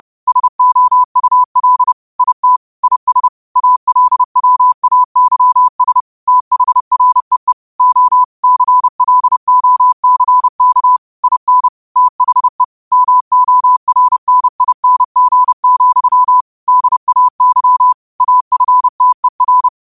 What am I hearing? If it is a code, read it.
Morse code, heard as a keyed tone, about 30 wpm, 1000 Hz. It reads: SOUL IT IS ALWAYS THREE O CLOCK IN THE MORNING, DAY AFTER